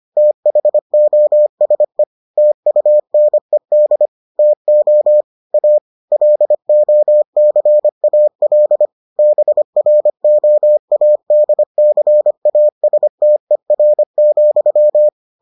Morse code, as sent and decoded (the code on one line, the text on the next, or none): - .... --- ... / - ..- -. -.. / - --- / .- / .-.. --- -.-. .- .-.. / -... .-. --- .- -.. -.-. .- ... - . .-. --..--
THOSE TUNED TO A LOCAL BROADCASTER,